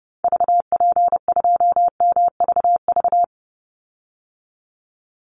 Morse code, heard as a keyed tone, about 30 wpm, 700 Hz. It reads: VP2MVV
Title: Call Signs - 30wpm